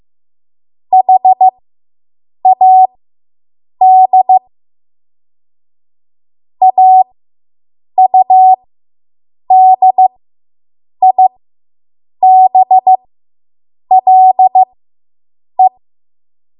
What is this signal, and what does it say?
Morse code, heard as a keyed tone, about 15 wpm, spaced as for 7 wpm, 750 Hz. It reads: HAD AUDIBLE